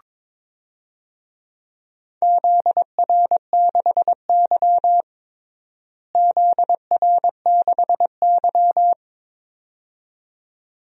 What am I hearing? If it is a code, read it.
Morse code, heard as a keyed tone, about 21 wpm, 700 Hz. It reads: ZR6Y ZR6Y
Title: DXCC-100 Call Signs - 4 Characters - Repeated 2x - 22wpm